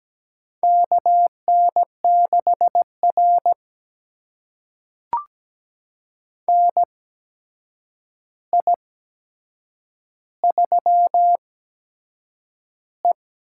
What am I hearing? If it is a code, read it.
Morse code, heard as a keyed tone, about 17 wpm, spaced as for 7 wpm, 700 Hz. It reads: KN6R NI3E
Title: USA Call Signs - 4 Characters - 8x Character Spacing - 17wpm